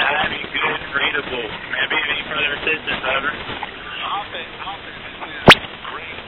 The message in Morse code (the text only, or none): none